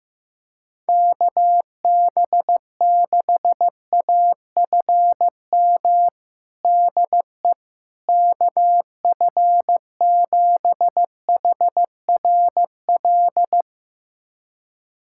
Morse code, heard as a keyed tone, about 16 wpm, 700 Hz. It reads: KB6AFM DE KF7HRL
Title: USA Callsign DE Callsign - 6 Characters - 15wpm